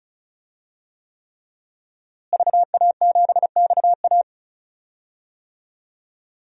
Morse code, heard as a keyed tone, about 34 wpm, 700 Hz. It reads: VA7XA
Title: DXCC-100 Call Signs - 5 Characters - Repeated 3x - 35wpm